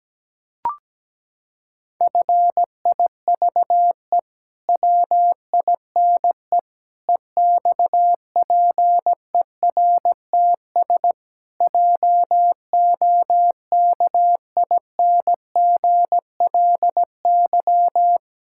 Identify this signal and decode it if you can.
Morse code, heard as a keyed tone, about 17 wpm, 700 Hz. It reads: FIVE WINE EXPERTS JOKINGLY
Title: Pangram Challenge v2 - 17wpm